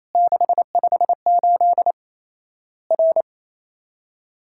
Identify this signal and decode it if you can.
Morse code, heard as a keyed tone, about 28 wpm, 700 Hz. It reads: 658 R
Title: Worked All Europe DX Contest - QTC Exchange - 28wpm